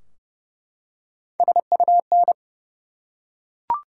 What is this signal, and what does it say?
Morse code, heard as a keyed tone, about 32 wpm, 700 Hz. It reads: SUN
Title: Three Letter Words - 30wpm